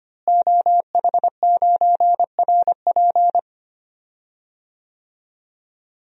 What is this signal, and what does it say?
Morse code, heard as a keyed tone, about 25 wpm, 700 Hz. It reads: OH9RP
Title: DXCC-100 Call Signs - 5 Characters - Repeated 2x - 25wpm